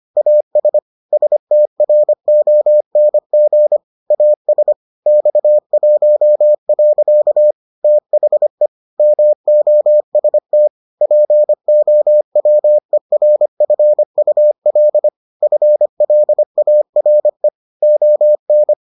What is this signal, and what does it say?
Morse code, heard as a keyed tone, about 25 wpm, 600 Hz. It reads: AS STRONG AS X1. THE MOST POWERFUL FLARE ON